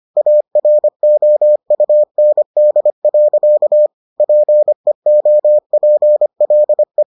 Morse code, heard as a keyed tone, about 25 wpm, 600 Hz. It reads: AROUND. PEOPLE